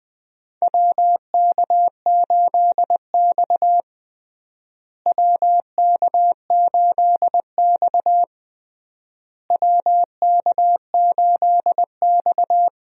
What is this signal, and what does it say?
Morse code, heard as a keyed tone, about 21 wpm, 700 Hz. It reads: WK8X WK8X WK8X
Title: USA Call Signs - 4 Characters - Repeated 3x - 20wpm